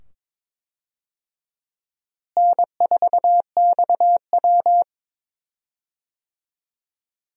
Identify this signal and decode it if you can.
Morse code, heard as a keyed tone, about 22 wpm, 700 Hz. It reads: N4XW
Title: USA Call Signs - 4 Characters - Repeated 2x - 22wpm